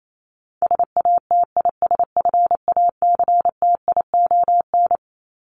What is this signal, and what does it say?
Morse code, heard as a keyed tone, about 28 wpm, 700 Hz. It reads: SATISFACTION
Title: Top 1751-2000 Words 28wpm